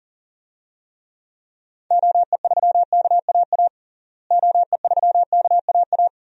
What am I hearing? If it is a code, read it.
Morse code, heard as a keyed tone, about 40 wpm, 700 Hz. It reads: OE3KAA OE3KAA
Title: DXCC-100 Call Signs - 6 Characters - Repeated 2x - 40wpm